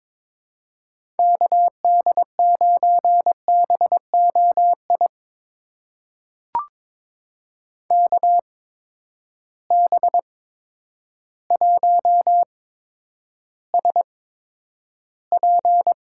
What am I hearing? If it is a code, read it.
Morse code, heard as a keyed tone, about 22 wpm, spaced as for 10 wpm, 700 Hz. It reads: KD9BOI KB1SP
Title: USA Call Signs - 6 Characters - 8x Character Spacing - 22wpm